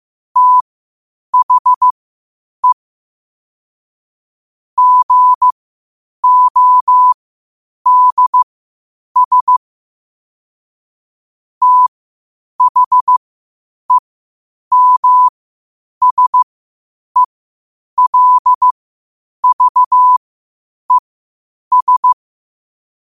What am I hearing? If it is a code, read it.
Morse code, heard as a keyed tone, about 15 wpm, spaced as for 8 wpm, 1000 Hz. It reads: THE GODS THEMSELVES
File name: qotdfiles / 8wpm